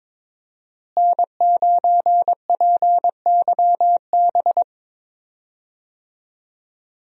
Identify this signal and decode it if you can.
Morse code, heard as a keyed tone, about 22 wpm, 700 Hz. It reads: N9PYB
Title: USA Call Signs - 5 Characters - Repeated 3x - 22wpm